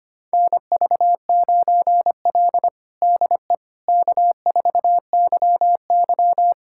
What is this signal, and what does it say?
Morse code, heard as a keyed tone, about 25 wpm, 700 Hz. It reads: NV9L DE K4YY
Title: USA Callsign DE Callsign - 4 Characters - 25wpm